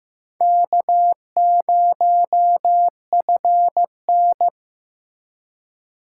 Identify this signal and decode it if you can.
Morse code, heard as a keyed tone, about 15 wpm, 700 Hz. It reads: K0FN
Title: USA Call Signs - 4 Characters - 4x Character Spacing - 15wpm